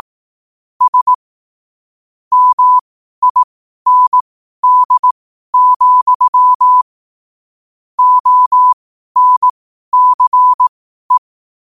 Morse code, heard as a keyed tone, about 18 wpm, spaced as for 12 wpm, 1000 Hz. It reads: S MIND, ONCE
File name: qotdfiles / 12wpm